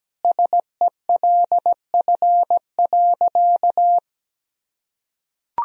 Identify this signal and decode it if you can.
Morse code, heard as a keyed tone, about 17 wpm, 700 Hz. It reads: SELF.
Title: Sentences from Top 1000 Words - Review Part 1 - 17wpm